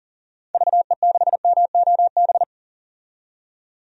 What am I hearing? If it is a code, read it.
Morse code, heard as a keyed tone, about 39 wpm, 700 Hz. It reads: VE6MOB
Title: DXCC-100 Call Signs - 6 Characters - 40wpm